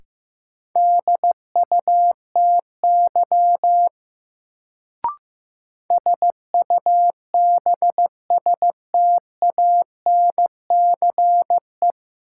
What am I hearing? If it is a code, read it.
Morse code, heard as a keyed tone, about 15 wpm, 700 Hz. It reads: DUTY SUBSTANCE
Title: Sentences from Top 1500 Words - New Word Drill Alt - 15wpm